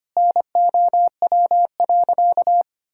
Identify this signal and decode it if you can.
Morse code, heard as a keyed tone, about 25 wpm, 700 Hz. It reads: NOW.
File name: practice / 25wpm / Sentences with 3-Word Phrases - 25wpm